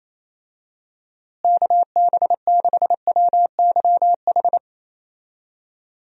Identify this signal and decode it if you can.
Morse code, heard as a keyed tone, about 28 wpm, 700 Hz. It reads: KB6WYH
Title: USA Call Signs - 6 Characters - Repeated 3x - 28wpm